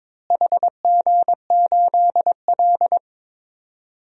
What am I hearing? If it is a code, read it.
Morse code, heard as a keyed tone, about 22 wpm, 700 Hz. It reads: HG8L
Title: DXCC-100 Call Signs - 4 Characters - Repeated 3x - 22wpm